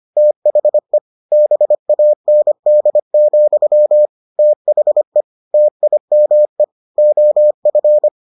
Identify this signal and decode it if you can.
Morse code, heard as a keyed tone, about 25 wpm, 600 Hz. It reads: THE BAND, THE TIME OF